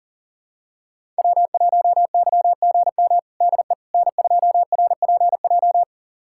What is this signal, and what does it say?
Morse code, heard as a keyed tone, about 37 wpm, 700 Hz. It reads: W1YGM DE N2RPJ
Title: USA Callsign DE Callsign - 5 Characters - 40wpm